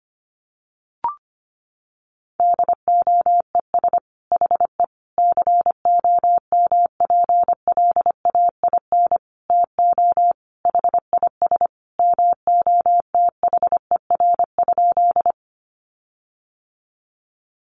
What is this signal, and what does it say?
Morse code, heard as a keyed tone, about 25 wpm, 700 Hz. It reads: DOES HE COMPLAIN TO HIS MOTHER?
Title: Sentences from Top 1300 Words 25wpm